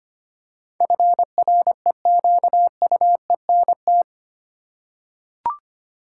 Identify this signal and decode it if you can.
Morse code, heard as a keyed tone, about 25 wpm, 700 Hz. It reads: FREQUENT
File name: 8 Letter Words - First 3 Letters Given - 25wpm